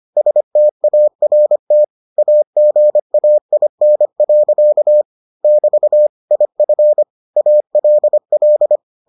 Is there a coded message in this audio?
Yes: Morse code from 0.2 to 8.8 s, heard as a keyed tone, about 25 wpm, 600 Hz. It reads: START AGAIN. <BT> IF ALL